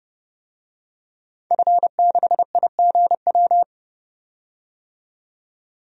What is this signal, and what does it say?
Morse code, heard as a keyed tone, about 30 wpm, 700 Hz. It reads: F6IGW
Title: DXCC-100 Call Signs - 5 Characters - 30wpm